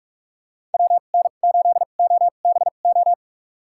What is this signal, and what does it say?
Morse code, heard as a keyed tone, about 43 wpm, spaced as for 31 wpm, 700 Hz. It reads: WN8OBO
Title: USA Call Signs - 6 Characters - 2x Character Spacing - 45wpm